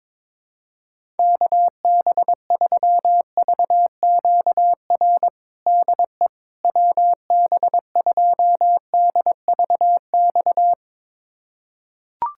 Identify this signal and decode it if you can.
Morse code, heard as a keyed tone, about 22 wpm, 700 Hz. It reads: KB3VQR DE WB2DVX
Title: USA Callsign DE Callsign - 6 Characters - 22wpm